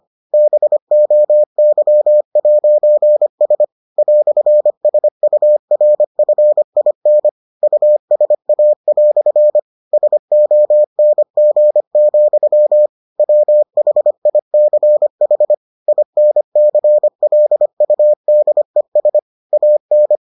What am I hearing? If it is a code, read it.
Morse code, heard as a keyed tone, about 25 wpm, 600 Hz. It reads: BOY'S "SURFIN USA" SONG, WHICH INCLUDES AN